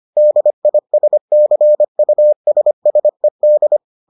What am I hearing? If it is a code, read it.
Morse code, heard as a keyed tone, about 25 wpm, 600 Hz. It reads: DISCUSSED